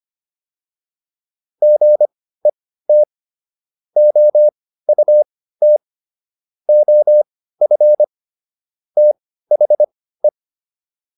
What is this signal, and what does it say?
Morse code, heard as a keyed tone, about 25 wpm, spaced as for 15 wpm, 600 Hz. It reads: GET OUT OF THE